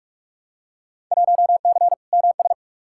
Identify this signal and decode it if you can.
Morse code, heard as a keyed tone, about 42 wpm, 700 Hz. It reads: 1C MS
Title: Field Day Exchanges Encore - 45wpm